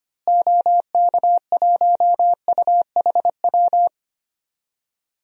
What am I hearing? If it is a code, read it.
Morse code, heard as a keyed tone, about 25 wpm, 700 Hz. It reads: OK1UHW
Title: DXCC-100 Call Signs - 6 Characters - Repeated 3x - 25wpm